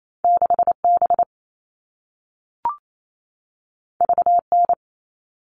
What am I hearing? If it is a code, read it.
Morse code, heard as a keyed tone, about 28 wpm, 700 Hz. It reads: 6B VN
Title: All Cut Numbers - 2-Digit 28wpm